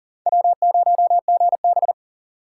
Morse code, heard as a keyed tone, about 37 wpm, 700 Hz. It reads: W0GB
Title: USA Call Signs - 3x Character Spacing - 40wpm